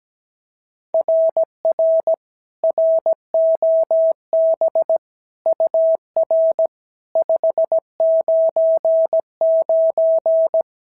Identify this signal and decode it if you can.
Morse code, heard as a keyed tone, about 15 wpm, 650 Hz. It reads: RR ROB UR 599
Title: SOTA QSO Practice 17wpm